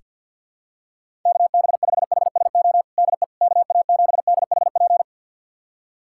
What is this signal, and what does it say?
Morse code, heard as a keyed tone, about 45 wpm, 700 Hz. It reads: KB5HSO DE KA7DHP
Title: USA Callsign DE Callsign 50wpm